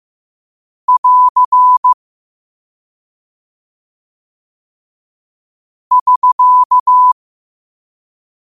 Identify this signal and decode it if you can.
Morse code, heard as a keyed tone, about 15 wpm, 1000 Hz. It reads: <AR> <SK>